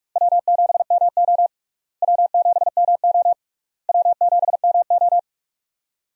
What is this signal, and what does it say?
Morse code, heard as a keyed tone, about 46 wpm, 700 Hz. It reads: W7MO W7MO W7MO
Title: USA Call Signs - 4 Characters - Repeated 3x - 45wpm